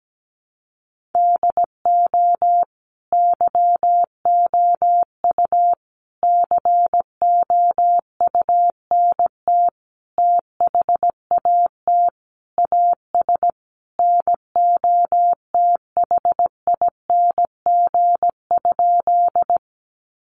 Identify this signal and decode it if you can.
Morse code, heard as a keyed tone, about 17 wpm, 700 Hz. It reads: DO YOU COUNT THAT AS NOTHING?
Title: Sentences from Top 1500 Words 17wpm